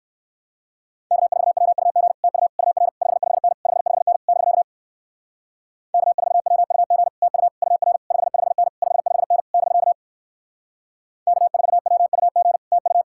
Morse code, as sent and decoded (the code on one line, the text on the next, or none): -.- ....- .-- ..- --. / - ..- / ..- .-. / ..... ..... -. / ..... ..... -. / -...-.- / -.- ....- .-- ..- --. / - ..- / ..- .-. / ..... ..... -. / ..... ..... -. / -...-.- / -.- ....- .-- ..- --. / - ..-
K4WUG TU UR 55N 55N <BK> K4WUG TU UR 55N 55N <BK> K4WUG TU